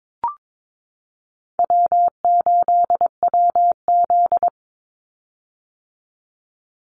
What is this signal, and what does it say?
Morse code, heard as a keyed tone, about 22 wpm, 700 Hz. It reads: W8WZ
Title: Call Signs - 22wpm